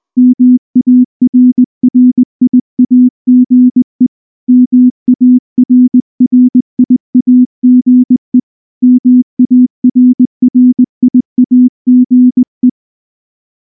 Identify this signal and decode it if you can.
Morse code, heard as a keyed tone, about 19 wpm, 250 Hz. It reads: MARRIAGE MARRIAGE MARRIAGE